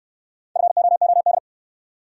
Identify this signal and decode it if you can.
Morse code, heard as a keyed tone, about 68 wpm, 700 Hz. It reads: FOOD